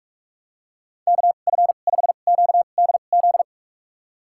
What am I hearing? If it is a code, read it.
Morse code, heard as a keyed tone, about 41 wpm, spaced as for 31 wpm, 700 Hz. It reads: KF5QDZ
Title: USA Call Signs - 6 Characters - 2x Character Spacing - 45wpm